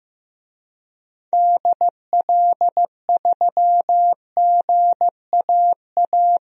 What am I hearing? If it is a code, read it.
Morse code, heard as a keyed tone, about 15 wpm, 700 Hz. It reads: DL3GAA
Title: DXCC-100 Call Signs - 6 Characters - Repeated 3x - 15wpm